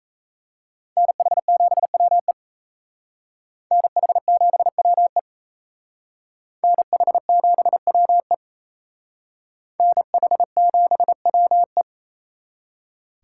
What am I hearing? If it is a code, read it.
Morse code, heard as a keyed tone, about 35 wpm, 700 Hz. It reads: NH7WE NH7WE NH7WE NH7WE
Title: Speed Racing - USA 2x2 Call Signs 28wpm